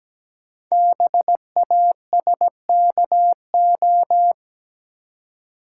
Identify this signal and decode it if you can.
Morse code, heard as a keyed tone, about 17 wpm, 700 Hz. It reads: BASKO
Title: Code Groups - 5 Letters - 17wpm